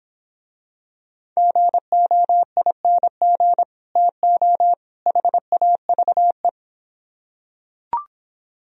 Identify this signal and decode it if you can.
Morse code, heard as a keyed tone, about 25 wpm, 700 Hz. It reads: GOING TO HAVE